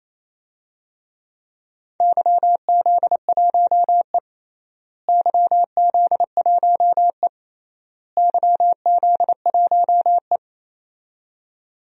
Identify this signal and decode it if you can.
Morse code, heard as a keyed tone, about 28 wpm, 700 Hz. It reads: YZ1E YZ1E YZ1E
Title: DXCC-100 Call Signs - 4 Characters - Repeated 3x - 28wpm